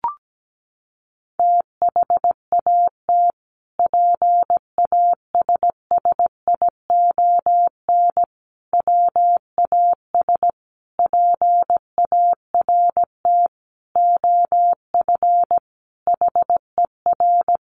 Morse code, heard as a keyed tone, about 17 wpm, 700 Hz. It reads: THAT PASSION WAS PART OF HER
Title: Sentences from Top 1300 Words 17wpm